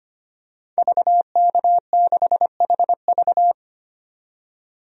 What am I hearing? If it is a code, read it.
Morse code, heard as a keyed tone, about 25 wpm, 700 Hz. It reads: VK6HV
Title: DXCC-100 Call Signs - 5 Characters - 25wpm